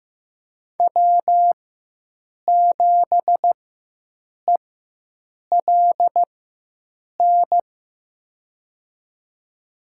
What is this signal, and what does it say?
Morse code, heard as a keyed tone, about 15 wpm, spaced as for 7 wpm, 700 Hz. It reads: W7ELN
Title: USA Call Signs - 5 Characters - 4x Character Spacing - 15wpm